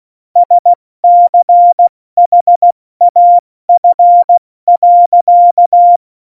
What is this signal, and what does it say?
Morse code, heard as a keyed tone, about 16 wpm, 700 Hz. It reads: SCHAF.
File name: Morsekurs / Witze / Witz 06. Oktober 2020, 70 F80